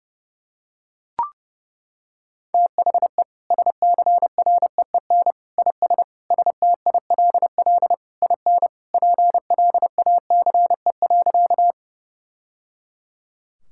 Morse code, heard as a keyed tone, about 30 wpm, 700 Hz. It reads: THE SCREEN IS STILL IN PLACE.
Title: Sentences from Top 1000 Words - Review Part 6 - 30wpm